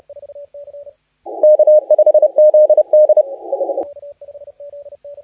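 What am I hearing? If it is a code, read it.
Morse code, heard as a keyed tone, about 30 wpm, 610 Hz. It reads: K5ZD
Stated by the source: Morse contest audio recorded by LA4C during the SAC CW 2018 contest.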